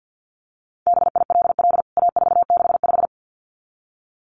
Morse code, heard as a keyed tone, about 49 wpm, 700 Hz. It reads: BILL A465
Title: CWT Exchange 50wpm